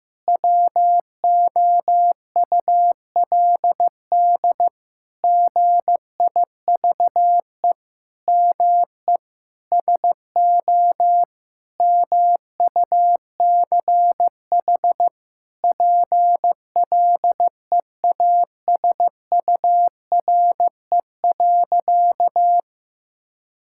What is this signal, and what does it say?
Morse code, heard as a keyed tone, about 14 wpm, 700 Hz. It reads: WOULD GIVE ME SO MUCH PLEASURE.
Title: Sentences from Top 1600 Words 15wpm